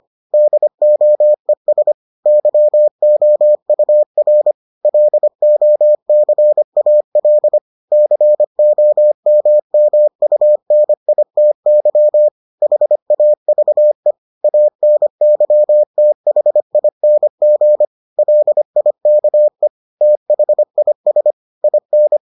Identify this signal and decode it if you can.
Morse code, heard as a keyed tone, about 25 wpm, 600 Hz. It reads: DOES YOUR LOCAL COMMUNITY HAVE ANYTHING LIKE THIS IN